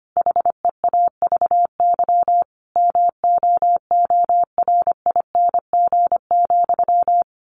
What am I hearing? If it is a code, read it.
Morse code, heard as a keyed tone, about 25 wpm, 700 Hz. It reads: HEAVY MOORING,
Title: A Sound Of Thunder 25wpm